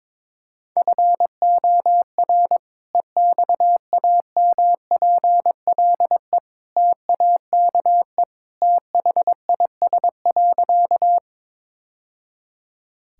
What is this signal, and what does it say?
Morse code, heard as a keyed tone, about 22 wpm, 700 Hz. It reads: FOR EXAMPLE TAKE THIS.
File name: practice / 22wpm / Sentences with 2-Word Phrases - 22wpm